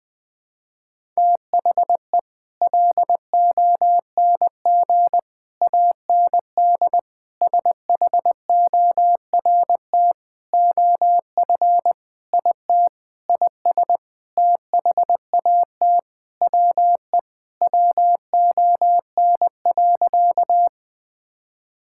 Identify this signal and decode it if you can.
Morse code, heard as a keyed tone, about 20 wpm, 700 Hz. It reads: THE LONG AND SHORT OF IT IS THAT WE WON.
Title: Sentences with 8-Word Phrases - Spoken Beginning - 20wpm